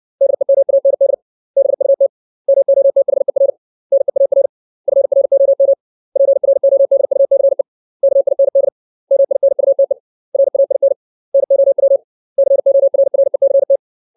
Morse code, heard as a keyed tone, about 62 wpm, 550 Hz. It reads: DEMAND BUT MOTHER NEAR FROM PRODUCE KIND MINUTE RAIN NOW CORRECT
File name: 632 words / 632 Words 60WPM